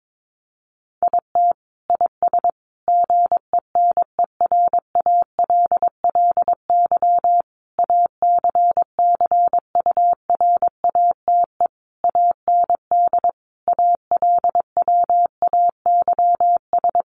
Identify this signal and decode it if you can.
Morse code, heard as a keyed tone, about 22 wpm, 700 Hz. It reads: IT IS GENERALLY ACCURATE AND ALWAYS
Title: Sentences from Top 1200 Words 22wpm